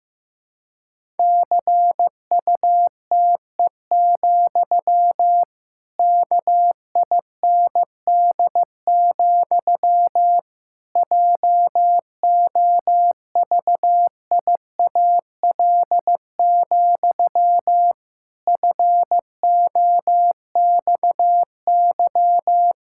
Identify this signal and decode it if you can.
Morse code, heard as a keyed tone, about 15 wpm, 700 Hz. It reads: CUTE, KIND, JOVIAL, FOXY
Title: Pangram Challenge v2 - 15wpm